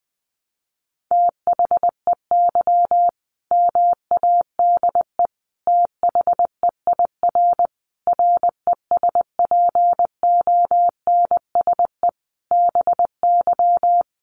Morse code, heard as a keyed tone, about 20 wpm, 700 Hz. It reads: THEY MADE THEIR RESPONSE BY
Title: Sentences from Top 1400 Words 20wpm